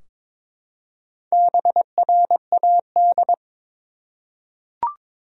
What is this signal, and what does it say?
Morse code, heard as a keyed tone, about 22 wpm, 700 Hz. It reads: BRAD
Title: USA Common Names - Max 4 Letters - 22wpm